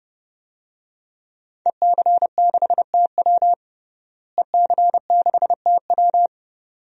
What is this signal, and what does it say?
Morse code, heard as a keyed tone, about 31 wpm, 700 Hz. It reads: EC6TW EC6TW
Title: DXCC-100 Call Signs - 5 Characters - Repeated 2x - 30wpm